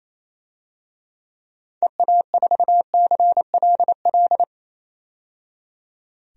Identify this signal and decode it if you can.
Morse code, heard as a keyed tone, about 27 wpm, 700 Hz. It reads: EA4CLL
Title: DXCC-100 Call Signs - 6 Characters - 28wpm